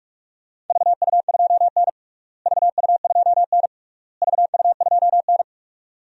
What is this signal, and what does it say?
Morse code, heard as a keyed tone, about 46 wpm, 700 Hz. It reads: VU2N VU2N VU2N
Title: DXCC-100 Call Signs - 4 Characters - Repeated 3x - 45wpm